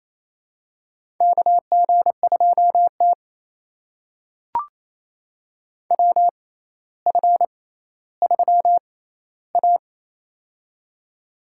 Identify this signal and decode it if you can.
Morse code, heard as a keyed tone, about 28 wpm, spaced as for 14 wpm, 700 Hz. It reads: KG2T WF3A
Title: USA Call Signs - 4 Characters - 6x Character Spacing - 28wpm